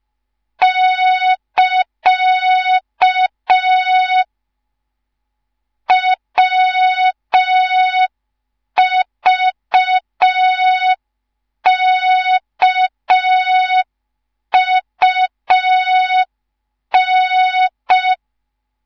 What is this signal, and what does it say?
Morse code, heard as a keyed tone, about 5 wpm, 3700 Hz. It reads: <KA> WVKUN